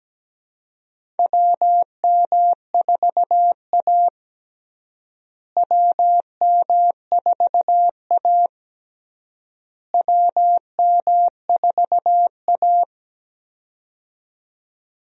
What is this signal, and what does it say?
Morse code, heard as a keyed tone, about 17 wpm, 700 Hz. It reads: WM4A WM4A WM4A
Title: USA Call Signs - 4 Characters - Repeated 3x - 17wpm